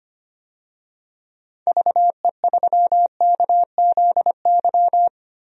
Morse code, heard as a keyed tone, about 25 wpm, 700 Hz. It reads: VE3KZY
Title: DXCC-100 Call Signs - 6 Characters - Repeated 2x - 25wpm